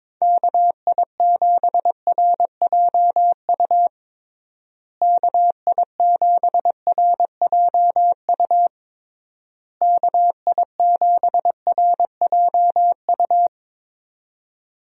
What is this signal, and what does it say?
Morse code, heard as a keyed tone, about 21 wpm, 700 Hz. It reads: KI7RJU KI7RJU KI7RJU
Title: USA Call Signs - 6 Characters - Repeated 3x - 22wpm